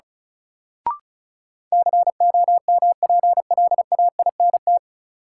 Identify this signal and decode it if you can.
Morse code, heard as a keyed tone, about 36 wpm, 700 Hz. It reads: COMPLAINT
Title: Sentences from Top 1400 Words - New Word Drill - 35wpm